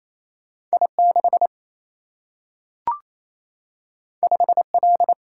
Morse code, heard as a keyed tone, about 28 wpm, 700 Hz. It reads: I 6 5 L
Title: Single-Letter Single-Number 28wpm